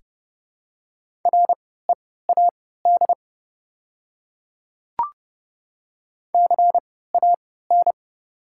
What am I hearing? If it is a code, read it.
Morse code, heard as a keyed tone, about 30 wpm, spaced as for 17 wpm, 690 Hz. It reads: READ CAN
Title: 3x Character Spacing - Top 200 Words - 30wpm